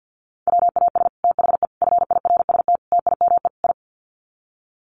Morse code, heard as a keyed tone, about 48 wpm, 700 Hz. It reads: WAS THE FIRST TIME I
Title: Sets of 5 Words Encore 50wpm